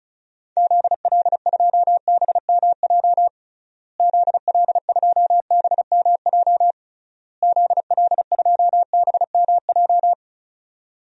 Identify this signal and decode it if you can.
Morse code, heard as a keyed tone, about 34 wpm, 700 Hz. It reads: ZL2BMJ ZL2BMJ ZL2BMJ
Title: DXCC-100 Call Signs - 6 Characters - Repeated 3x - 35wpm